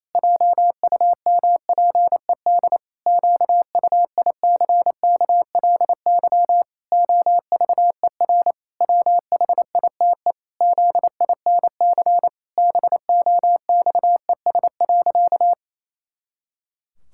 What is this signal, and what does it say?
Morse code, heard as a keyed tone, about 28 wpm, 700 Hz. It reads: JUMPED QUICKLY OVER WHITE ZINC BOXES.
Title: Pangram Challenge v2 - 28wpm